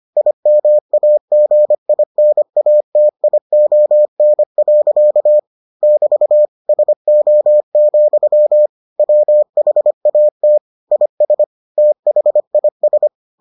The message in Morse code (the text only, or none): IMAGINATION. <BT> SO, WHAT IS THIS